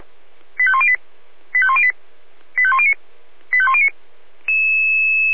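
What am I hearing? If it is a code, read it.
Five-tone selective call (ZVEI), heard as quick runs of five tones, tones of about 70 ms. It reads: 85108 85108 85109 85109
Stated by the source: ZVEI five-tone call 85300 received on an SDR.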